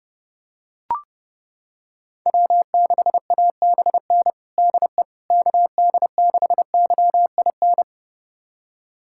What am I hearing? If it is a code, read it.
Morse code, heard as a keyed tone, about 30 wpm, 700 Hz. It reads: W6ABN DE KD6YIN
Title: USA Callsign DE Callsign 30wpm